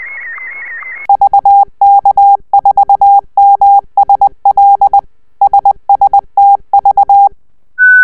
Morse code, heard as a keyed tone, about 20 wpm, 800 Hz. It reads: VK4MSL SSTV